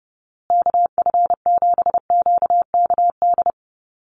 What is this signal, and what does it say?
Morse code, heard as a keyed tone, about 30 wpm, 700 Hz. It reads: KF7QKD
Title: USA Call Signs Repeated 2x 30wpm